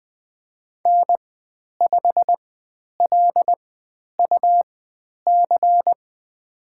Morse code, heard as a keyed tone, about 20 wpm, spaced as for 10 wpm, 700 Hz. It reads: N5LUC
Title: 102 Lesson 17g - Letter u - Call Signs 20wpm